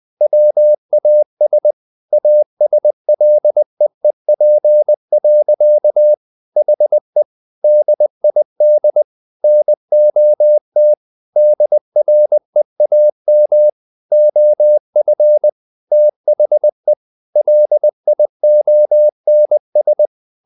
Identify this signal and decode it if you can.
Morse code, heard as a keyed tone, about 20 wpm, 600 Hz. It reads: WAS ASLEEP. HE DID NOT DREAM OF THE LIONS
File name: the old man and the sea part 2 20wpm